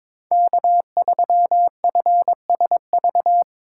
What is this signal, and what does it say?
Morse code, heard as a keyed tone, about 22 wpm, 700 Hz. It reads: K3FSV
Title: USA Call Signs - 5 Characters - 4x Character Spacing - 22wpm